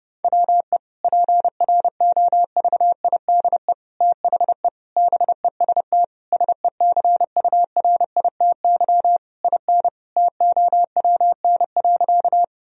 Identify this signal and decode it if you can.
Morse code, heard as a keyed tone, about 29 wpm, 700 Hz. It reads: WE PROVIDE THE BEST SECURITY IN TOWN.
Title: Sentences from Top 1000 Words - Review Part 4 - 30wpm